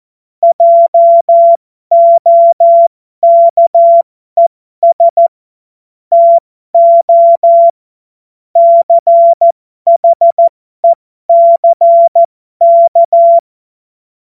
Morse code, heard as a keyed tone, about 14 wpm, spaced as for 12 wpm, 680 Hz. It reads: JOKES TO CHECK
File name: Morsekurs / Witze